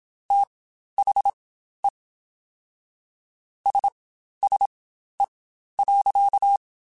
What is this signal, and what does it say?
Morse code, heard as a keyed tone, about 27 wpm, spaced as for 12 wpm, 800 Hz. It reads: THE SSE.
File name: Spaced Characters / Spaced Characters-mp3s / Weather4 14 sp @25